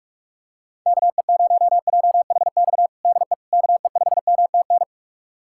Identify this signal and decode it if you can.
Morse code, heard as a keyed tone, about 46 wpm, 700 Hz. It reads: KE0JHX DE KE5MTN
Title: USA Callsign DE Callsign - 6 Characters - 45wpm